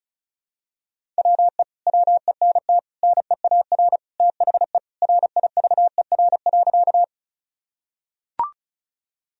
Morse code, heard as a keyed tone, about 34 wpm, 700 Hz. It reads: WE WENT NEAR THE RIVER.
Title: Sentences from Top 1000 Words - Review Part 1 - 35wpm